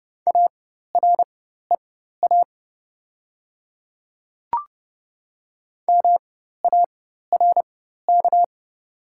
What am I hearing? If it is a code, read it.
Morse code, heard as a keyed tone, about 30 wpm, spaced as for 14 wpm, 700 Hz. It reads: AREA MARK
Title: 4x Character Spacing - Top 500 Words - 30wpm